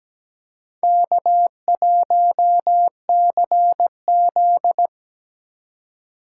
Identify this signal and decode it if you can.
Morse code, heard as a keyed tone, about 17 wpm, 700 Hz. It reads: K1CZ
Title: USA Call Signs - 4 Characters - 4x Character Spacing - 17wpm